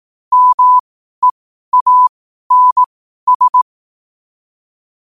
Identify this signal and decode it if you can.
Morse code, heard as a keyed tone, about 18 wpm, spaced as for 12 wpm, 1000 Hz. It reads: MEANS
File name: qotdfiles / 12wpm